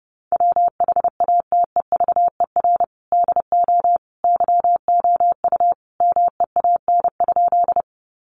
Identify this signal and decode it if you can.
Morse code, heard as a keyed tone, about 30 wpm, 700 Hz. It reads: WHATEVER DO YOU MEAN?
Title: Sentences from Top 1100 Words 30wpm